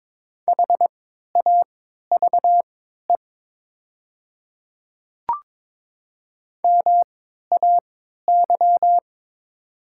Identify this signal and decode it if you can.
Morse code, heard as a keyed tone, about 22 wpm, spaced as for 12 wpm, 700 Hz. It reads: HAVE MAY
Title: 3x Character Spacing - Top 200 Words - 22wpm